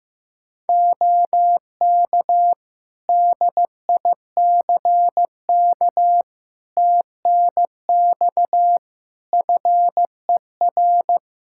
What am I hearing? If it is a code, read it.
Morse code, heard as a keyed tone, about 15 wpm, 700 Hz. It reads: OK DICK TNX FER
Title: Practice QSO - Exchange 4 - Ending - 15wpm